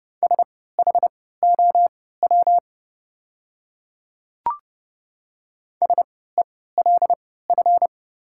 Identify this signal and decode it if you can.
Morse code, heard as a keyed tone, about 30 wpm, spaced as for 17 wpm, 700 Hz. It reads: SHOW SELF
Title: 3x Character Spacing - Top 200 Words - 30wpm